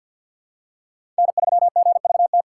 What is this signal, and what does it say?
Morse code, heard as a keyed tone, about 53 wpm, 700 Hz. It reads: N3GVT
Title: USA Call Signs - 5 Characters - 8x Character Spacing - 50wpm